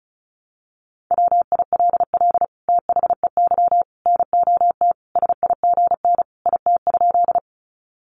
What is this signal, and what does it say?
Morse code, heard as a keyed tone, about 34 wpm, 700 Hz. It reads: WILL THEY NOT SIGN IT?
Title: Sentences from Top 1000 Words 35wpm